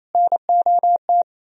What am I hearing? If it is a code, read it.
Morse code, heard as a keyed tone, about 36 wpm, 700 Hz. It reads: NOT